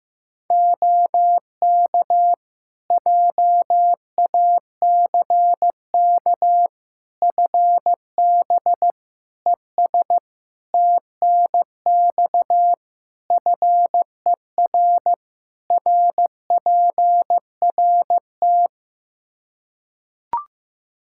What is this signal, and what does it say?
Morse code, heard as a keyed tone, about 15 wpm, 700 Hz. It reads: OK JACK FB ES TNX FER RPRT
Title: Practice QSO - Exchange 2 - Info - 15wpm